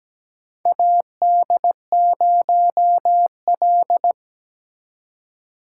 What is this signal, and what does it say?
Morse code, heard as a keyed tone, about 17 wpm, 700 Hz. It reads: AD0L
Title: USA Call Signs - 4 Characters - 3x Character Spacing - 17wpm